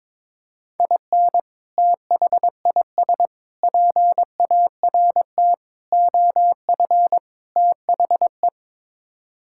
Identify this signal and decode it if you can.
Morse code, heard as a keyed tone, about 22 wpm, 700 Hz. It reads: IN THIS PART OF THE
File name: practice / 22wpm / Build A Phrase - 5-Word Phrases - 22wpm